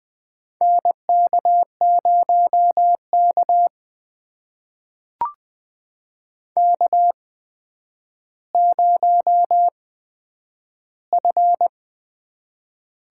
Morse code, heard as a keyed tone, about 21 wpm, spaced as for 10 wpm, 700 Hz. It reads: NK0K K0F
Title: USA Call Signs - 8x Character Spacing - 20wpm